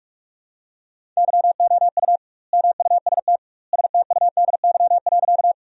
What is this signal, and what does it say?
Morse code, heard as a keyed tone, about 46 wpm, 700 Hz. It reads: YOU MUST STUDY.
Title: 2x Word-Spacing - Sentences from Top 300 Words - 45wpm